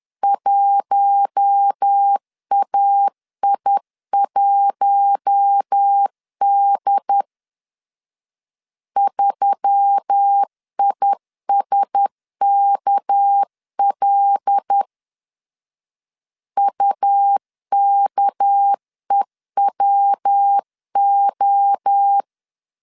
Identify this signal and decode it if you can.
Morse code, heard as a keyed tone, about 10 wpm, 800 Hz. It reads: 1AI1D 3ISKL UKEWO
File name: CW / DET / 1 6 1500C S B 10wpm 4137